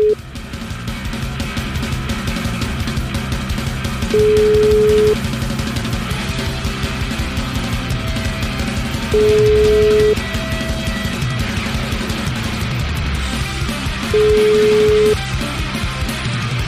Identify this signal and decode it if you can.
Ringback tone (Europe, 425 Hz), with noise.